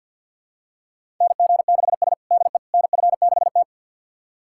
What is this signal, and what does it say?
Morse code, heard as a keyed tone, about 51 wpm, 700 Hz. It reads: NG6S DE NF6T
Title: USA Callsign DE Callsign - 4 Characters - 50wpm